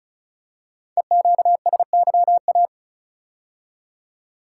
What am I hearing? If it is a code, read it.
Morse code, heard as a keyed tone, about 36 wpm, 700 Hz. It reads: EQSYA
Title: Code Groups - 5 Letters - 35wpm